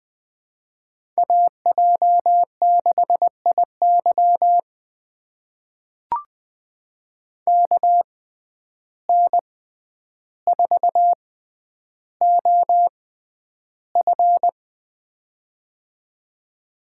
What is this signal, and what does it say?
Morse code, heard as a keyed tone, about 21 wpm, spaced as for 10 wpm, 700 Hz. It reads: AJ6IY KN4OF
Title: USA Call Signs - 5 Characters - 6x Character Spacing - 20wpm